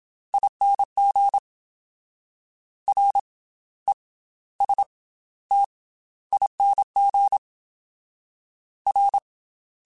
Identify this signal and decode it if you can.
Morse code, heard as a keyed tone, about 27 wpm, spaced as for 15 wpm, 800 Hz. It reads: ING RESTING R
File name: Special INGs / Special INGs-mp3s / Slow ing @25